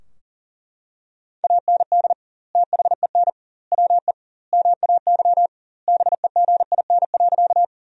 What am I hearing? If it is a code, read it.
Morse code, heard as a keyed tone, about 37 wpm, 700 Hz. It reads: AND THEN WE MAY BEGIN.
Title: 2x Word-Spacing - Sentences from Top 300 Words - 40wpm